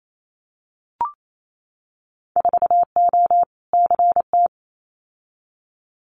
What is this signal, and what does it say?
Morse code, heard as a keyed tone, about 28 wpm, 700 Hz. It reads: 4O CT
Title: Winter Field Day Exchanges Encore 28wpm